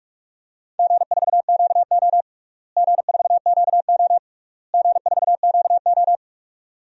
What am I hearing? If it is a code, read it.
Morse code, heard as a keyed tone, about 43 wpm, 700 Hz. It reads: G4QO G4QO G4QO
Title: DXCC-100 Call Signs - 4 Characters - Repeated 3x - 45wpm